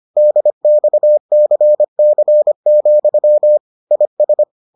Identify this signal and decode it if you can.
Morse code, heard as a keyed tone, about 25 wpm, 600 Hz. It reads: DXCC, IS